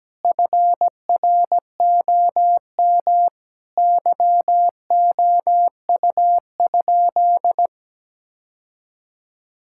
Morse code, heard as a keyed tone, about 17 wpm, 700 Hz. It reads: FROM YOU?
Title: Sentences from Top 1000 Words - Review Part 5 - 17wpm